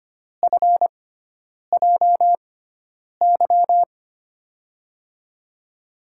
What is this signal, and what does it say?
Morse code, heard as a keyed tone, about 25 wpm, spaced as for 9 wpm, 700 Hz. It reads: FJY
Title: USA Call Signs - 6x Character Spacing - 25wpm